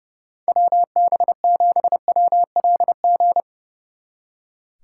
Morse code, heard as a keyed tone, about 38 wpm, 700 Hz. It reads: WB7WLG